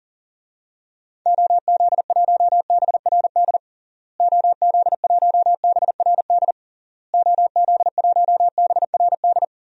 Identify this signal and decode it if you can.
Morse code, heard as a keyed tone, about 37 wpm, 700 Hz. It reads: OZ1BRD OZ1BRD OZ1BRD
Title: DXCC-100 Call Signs - 6 Characters - Repeated 3x - 40wpm